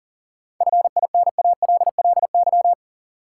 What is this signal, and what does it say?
Morse code, heard as a keyed tone, about 40 wpm, 700 Hz. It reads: FINALLY
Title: Sentences from Top 1000 Words - New Word Drill - 40wpm